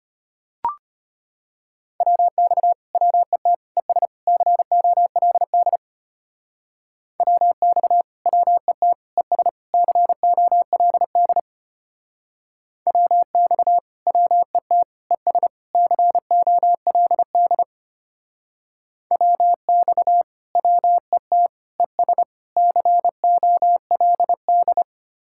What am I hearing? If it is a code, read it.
Morse code, heard as a keyed tone, about 31 wpm, 700 Hz. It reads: WX WET ES COLD WX WET ES COLD WX WET ES COLD WX WET ES COLD
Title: Speed-Racing Common QSO Phrases Encore - 25wpm